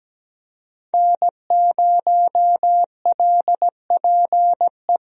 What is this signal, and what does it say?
Morse code, heard as a keyed tone, about 17 wpm, 700 Hz. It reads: N0LPE
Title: USA Call Signs - 5 Characters - Repeated 2x - 17wpm